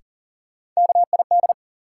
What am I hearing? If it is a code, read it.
Morse code, heard as a keyed tone, about 40 wpm, 700 Hz. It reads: KID
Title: Sets of 5 Words - New Word Drill - 40wpm